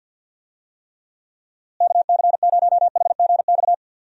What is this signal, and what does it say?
Morse code, heard as a keyed tone, about 51 wpm, 700 Hz. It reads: KC0HGX
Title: USA Call Signs - 6 Characters - Repeated 3x - 50wpm